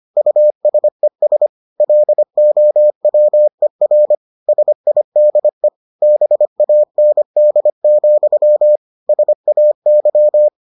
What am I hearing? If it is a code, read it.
Morse code, heard as a keyed tone, about 25 wpm, 600 Hz. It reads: USES LOWER SIDE BAND, SAY